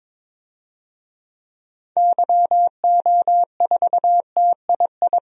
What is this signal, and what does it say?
Morse code, heard as a keyed tone, about 23 wpm, 700 Hz. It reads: YO4TII
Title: DXCC-100 Call Signs - 6 Characters - 22wpm